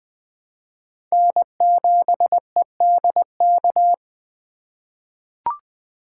Morse code, heard as a keyed tone, about 21 wpm, 700 Hz. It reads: N7EDK
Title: 2023 SOTA Chaser Call Signs - 20wpm